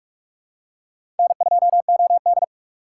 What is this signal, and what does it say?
Morse code, heard as a keyed tone, about 42 wpm, 700 Hz. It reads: N2OD
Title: USA Call Signs - 4 Characters - 3x Character Spacing - 45wpm